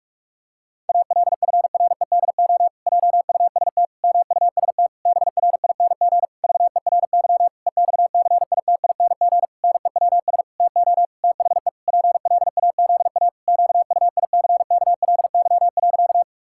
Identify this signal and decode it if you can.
Morse code, heard as a keyed tone, about 41 wpm, 700 Hz. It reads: ALFREDO JUST MUST BRING VERY EXCITING NEWS TO THE PLAZA QUICKLY.
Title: Pangram Challenge v2 - 45wpm